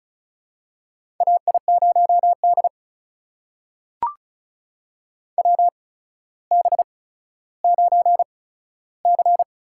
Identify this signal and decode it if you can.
Morse code, heard as a keyed tone, about 36 wpm, spaced as for 15 wpm, 700 Hz. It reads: AI0D WB9C
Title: USA Call Signs - 4 Characters - 8x Character Spacing - 35wpm